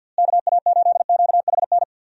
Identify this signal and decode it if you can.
Morse code, heard as a keyed tone, about 50 wpm, 700 Hz. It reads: KA8QHN
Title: USA Call Signs - 6 Characters - Repeated 3x - 50wpm